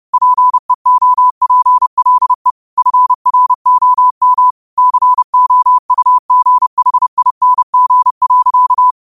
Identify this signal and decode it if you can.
Morse code, heard as a keyed tone, about 30 wpm, 1000 Hz. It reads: PEOPLE FROM COUGHING.